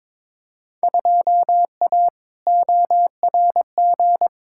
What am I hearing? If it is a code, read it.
Morse code, heard as a keyed tone, about 22 wpm, 700 Hz. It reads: 2A ORG
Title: Field Day Exchanges Encore - 22wpm